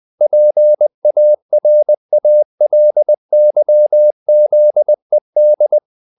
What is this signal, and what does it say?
Morse code, heard as a keyed tone, about 20 wpm, 600 Hz. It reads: PARALYZED
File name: the old man and the sea part 2 20wpm